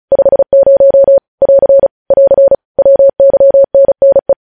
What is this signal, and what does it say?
Morse code, heard as a keyed tone, about 36 wpm, 550 Hz. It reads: 50 <AR> <AR> WYNNE
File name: podcast / News headlines-1x-35x35